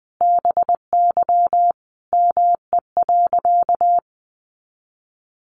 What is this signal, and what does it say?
Morse code, heard as a keyed tone, about 19 wpm, 700 Hz. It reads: BY ME.
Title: Sentences from Top 500 Words 20wpm